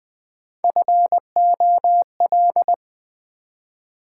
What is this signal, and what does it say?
Morse code, heard as a keyed tone, about 20 wpm, 700 Hz. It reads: FOL
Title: Build A Word - 1 Letter At A Time - 4-Letter Words - 20wpm